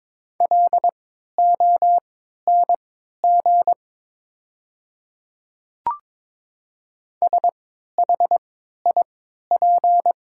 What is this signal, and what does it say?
Morse code, heard as a keyed tone, about 23 wpm, spaced as for 13 wpm, 700 Hz. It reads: LONG SHIP
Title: 3x Character Spacing - Top 500 Words - 22wpm